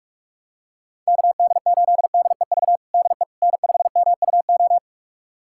Morse code, heard as a keyed tone, about 46 wpm, 700 Hz. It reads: KD8DEV DE N5MUO
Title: USA Callsign DE Callsign 45wpm